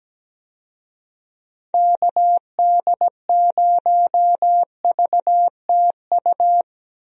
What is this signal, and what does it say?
Morse code, heard as a keyed tone, about 17 wpm, 700 Hz. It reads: KD0VTU
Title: USA Call Signs - 6 Characters - Repeated 3x - 17wpm